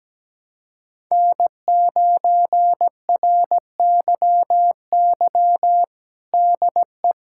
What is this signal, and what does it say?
Morse code, heard as a keyed tone, about 17 wpm, 700 Hz. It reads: N9RYY DE
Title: USA Callsign DE Callsign 17wpm